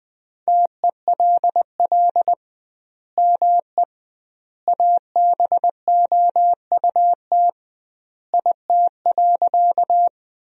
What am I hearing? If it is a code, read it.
Morse code, heard as a keyed tone, about 20 wpm, 700 Hz. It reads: TELL ME ABOUT IT.
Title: 2x Word-Spacing - Sentences from Top 200 Words - 20wpm